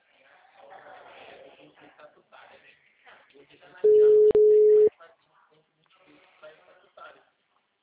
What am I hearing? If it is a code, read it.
Ringback tone (Europe, 425 Hz), with speech.